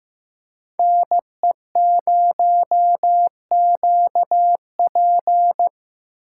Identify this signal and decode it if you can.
Morse code, heard as a keyed tone, about 15 wpm, 700 Hz. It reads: N E 0 Q P
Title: USA Call Signs - 8x Character Spacing - 15wpm